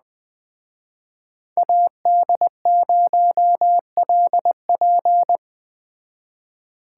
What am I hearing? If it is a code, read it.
Morse code, heard as a keyed tone, about 20 wpm, 700 Hz. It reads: AD0LP